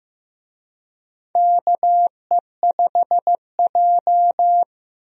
Morse code, heard as a keyed tone, about 16 wpm, 700 Hz. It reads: KE5J
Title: USA Call Signs - 4 Characters - 15wpm